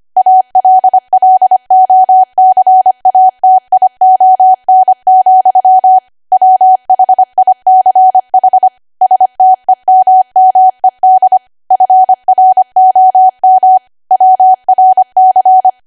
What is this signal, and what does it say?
Morse code, heard as a keyed tone, about 25 wpm, 750 Hz. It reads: ALLOCATION, WHICH STEMMED FROM WRC
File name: cwa / W1AW-Level3 / 25WPM